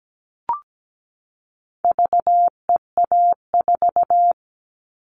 Morse code, heard as a keyed tone, about 17 wpm, 700 Hz. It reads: VEA4
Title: All Cut Numbers - 4-Digit 17wpm